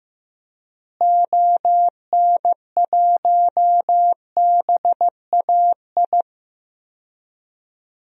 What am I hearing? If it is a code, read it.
Morse code, heard as a keyed tone, about 15 wpm, 700 Hz. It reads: ON1BAI
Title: DXCC-100 Call Signs - 6 Characters - Repeated 2x - 15wpm